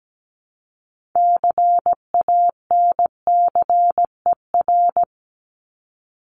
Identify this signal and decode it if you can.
Morse code, heard as a keyed tone, about 17 wpm, 700 Hz. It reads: CANCER
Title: Top 1251-1500 Words 17wpm